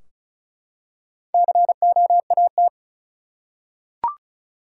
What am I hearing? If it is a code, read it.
Morse code, heard as a keyed tone, about 36 wpm, 700 Hz. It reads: COAT
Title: Four Letter Words - 35wpm